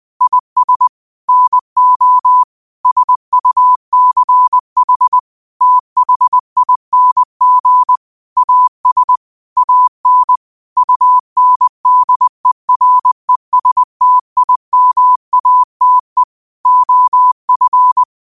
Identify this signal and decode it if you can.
Morse code, heard as a keyed tone, about 20 wpm, 1000 Hz. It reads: IS NO SUCH THING AS AN UNDERESTIMATE OF
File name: qotdfiles / 20wpm